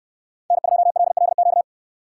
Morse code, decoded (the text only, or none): N2LLQ